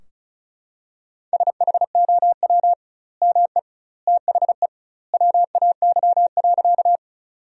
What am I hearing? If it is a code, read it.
Morse code, heard as a keyed tone, about 34 wpm, 700 Hz. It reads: SHOW ME THE WAY.
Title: 2x Word-Spacing - Sentences from Top 200 Words - 35wpm